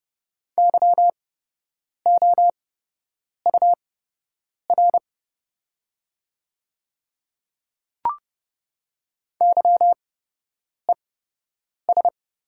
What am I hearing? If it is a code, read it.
Morse code, heard as a keyed tone, about 30 wpm, spaced as for 8 wpm, 700 Hz. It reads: YOUR YES
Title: 8x Character Spacing - Top 500 Words - 30wpm